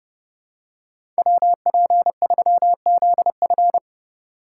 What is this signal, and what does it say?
Morse code, heard as a keyed tone, about 30 wpm, 700 Hz. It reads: WP3ZF
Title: USA Call Signs - 5 Characters - 2x Character Spacing - 30wpm